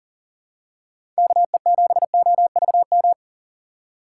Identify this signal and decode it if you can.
Morse code, heard as a keyed tone, about 40 wpm, 700 Hz. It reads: KE7OVM
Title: USA Call Signs - 6 Characters - 4x Character Spacing - 40wpm